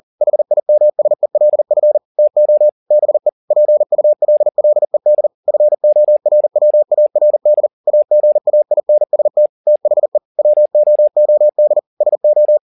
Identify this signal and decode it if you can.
Morse code, heard as a keyed tone, about 40 wpm, 600 Hz. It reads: HIMSELF TO BE PULLED FORWARD AGAINST THE WOOD SO
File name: the old man and the sea 2 40wpm